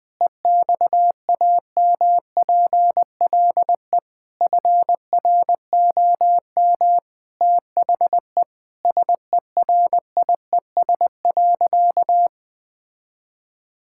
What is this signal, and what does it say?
Morse code, heard as a keyed tone, about 19 wpm, 700 Hz. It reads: EXAMPLE FROM THE SERIES.
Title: Sentences from Top 1000 Words - Review Part 5 - 20wpm